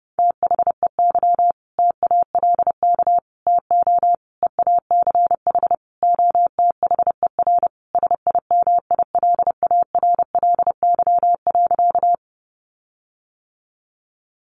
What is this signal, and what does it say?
Morse code, heard as a keyed tone, about 30 wpm, 700 Hz. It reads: THEY TALK TO EACH OTHER SIMILARLY.
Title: Sentences from Top 1400 Words 30wpm